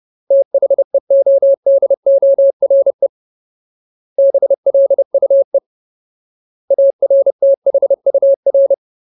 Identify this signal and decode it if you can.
Morse code, heard as a keyed tone, about 30 wpm, 550 Hz. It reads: THEODORE BLUE ARTHUR
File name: Ultimate Word List / 30WPM Ultimate